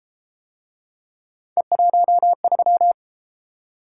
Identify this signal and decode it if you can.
Morse code, heard as a keyed tone, about 31 wpm, 700 Hz. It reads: E13